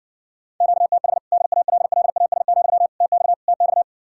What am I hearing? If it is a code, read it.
Morse code, heard as a keyed tone, about 60 wpm, 700 Hz. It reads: QTH DALLAS, TX TX